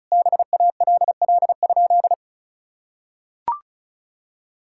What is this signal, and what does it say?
Morse code, heard as a keyed tone, about 34 wpm, 700 Hz. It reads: BALL?
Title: Sentences from Top 1000 Words - Review Part 6 - 35wpm